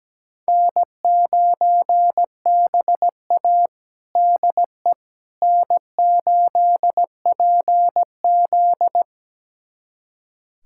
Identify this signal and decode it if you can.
Morse code, heard as a keyed tone, about 17 wpm, 700 Hz. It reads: N9BA DE N8PZ
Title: USA Callsign DE Callsign - 4 Characters - 17wpm